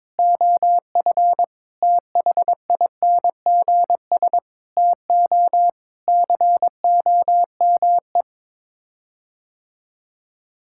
Morse code, heard as a keyed tone, about 22 wpm, 700 Hz. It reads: OF THINGS TO COME
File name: practice / 22wpm / Sets of 6 Words - 22wpm